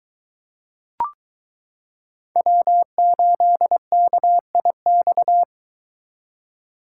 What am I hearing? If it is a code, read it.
Morse code, heard as a keyed tone, about 22 wpm, 700 Hz. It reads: W8KIX